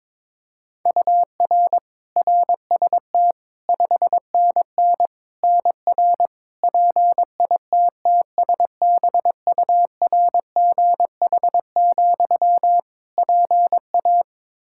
Morse code, heard as a keyed tone, about 22 wpm, 700 Hz. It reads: UR RST 5NN NR PITTSBURGH, PA
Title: Practice QSO - Exchange 1 - Beginning - 22wpm